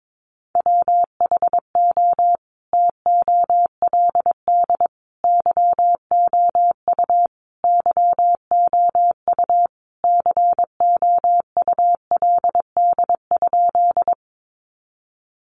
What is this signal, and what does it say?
Morse code, heard as a keyed tone, about 22 wpm, 700 Hz. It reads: WHO TOLD YOU YOU COULD?
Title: Sentences from Top 400 Words 22wpm